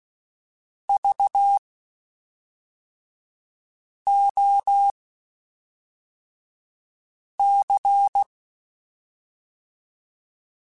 Morse code, heard as a keyed tone, about 16 wpm, spaced as for 3 wpm, 800 Hz. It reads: VOC
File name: QSOs / QSOs-mp3s / WD0BGP 1@5